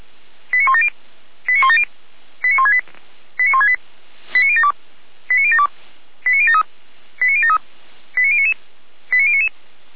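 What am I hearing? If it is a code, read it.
Five-tone selective call (ZVEI), heard as quick runs of five tones, tones of about 70 ms. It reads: 89170 89170 89168 89168 89072 89072 89073 89073 89099 89099
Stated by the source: Five-tone ZVEI selective call, received with an SDR.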